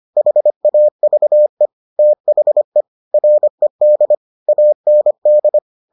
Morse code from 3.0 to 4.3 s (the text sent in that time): RED